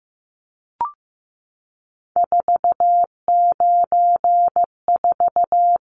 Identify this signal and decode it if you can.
Morse code, heard as a keyed tone, about 15 wpm, 700 Hz. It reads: 494
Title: Numbers - 3-Digit 15wpm